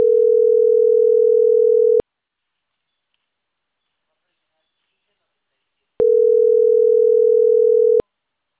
Ringback tone (North America), 440 + 480 Hz.